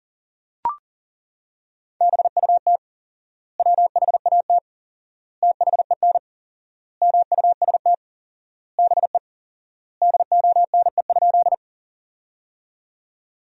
Morse code, heard as a keyed tone, about 40 wpm, 700 Hz. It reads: BUT WHAT THEN MUST BE DONE?
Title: 4x Word-Spacing - Sentences from Top 500 Words - 40wpm